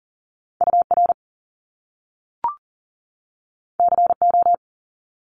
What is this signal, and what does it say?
Morse code, heard as a keyed tone, about 40 wpm, 700 Hz. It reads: UR CO
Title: Top Two Letter Combinations in Words 40wpm